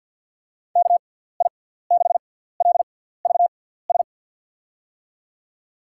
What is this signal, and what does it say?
Morse code, heard as a keyed tone, about 52 wpm, spaced as for 17 wpm, 700 Hz. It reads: KI6LVS
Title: USA Call Signs - 6 Characters - 6x Character Spacing - 50wpm